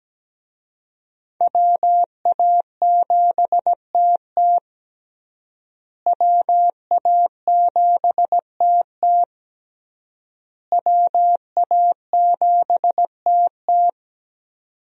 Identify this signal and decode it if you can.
Morse code, heard as a keyed tone, about 17 wpm, 700 Hz. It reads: WA7TT WA7TT WA7TT
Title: USA Call Signs - 5 Characters - Repeated 3x - 17wpm